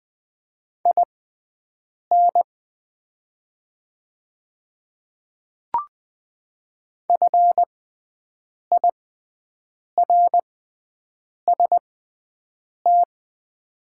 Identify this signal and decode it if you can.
Morse code, heard as a keyed tone, about 20 wpm, spaced as for 7 wpm, 700 Hz. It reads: IN FIRST
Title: 6x Character Spacing - Top 300 Words - 20wpm